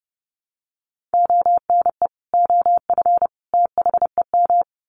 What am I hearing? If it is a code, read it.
Morse code, heard as a keyed tone, about 30 wpm, 700 Hz. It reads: ONE OF THEM
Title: Sets of 3 Words 30wpm